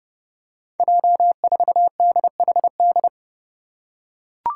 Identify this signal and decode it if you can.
Morse code, heard as a keyed tone, about 30 wpm, 700 Hz. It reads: J4DHD
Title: Five Challenging Letter-Number Combinations 30wpm